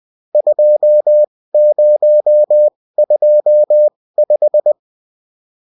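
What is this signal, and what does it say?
Morse code, heard as a keyed tone, about 20 wpm, spaced as for 16 wpm, 600 Hz. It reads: 2025
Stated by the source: Bulletin in text and CW form.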